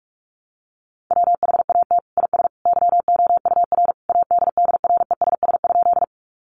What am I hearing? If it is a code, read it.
Morse code, heard as a keyed tone, about 44 wpm, 700 Hz. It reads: WHAT IS YOUR ADDRESS?
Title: Sentences from Top 1000 Words 45wpm